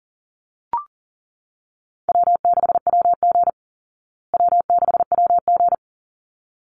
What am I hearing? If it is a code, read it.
Morse code, heard as a keyed tone, about 40 wpm, 700 Hz. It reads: W6WG W6WG
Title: Call Signs Repeated 3x - 40wpm